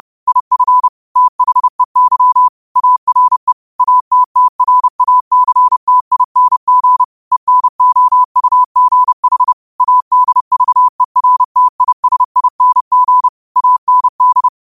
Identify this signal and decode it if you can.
Morse code, heard as a keyed tone, about 30 wpm, 1000 Hz. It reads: IF THEY ARE ATTRACTING ENOUGH ADVERTISING AND